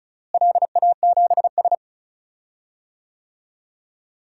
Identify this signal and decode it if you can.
Morse code, heard as a keyed tone, about 36 wpm, 700 Hz. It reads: LA7S
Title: DXCC-100 Call Signs - 4 Characters - Repeated 3x - 35wpm